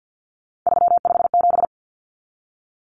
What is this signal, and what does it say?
Morse code, heard as a keyed tone, about 50 wpm, 700 Hz. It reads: 357
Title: Common Cut Numbers - 3-Digit 50wpm